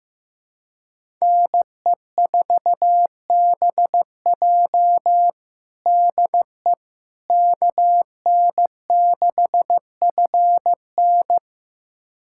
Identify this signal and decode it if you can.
Morse code, heard as a keyed tone, about 15 wpm, 700 Hz. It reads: NE4BJ DE KN6FN
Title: USA Callsign DE Callsign - 5 Characters - 15wpm